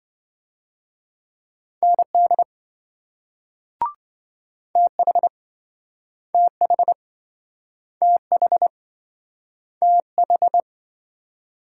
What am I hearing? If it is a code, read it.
Morse code, heard as a keyed tone, about 27 wpm, 700 Hz. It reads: ND TH TH TH TH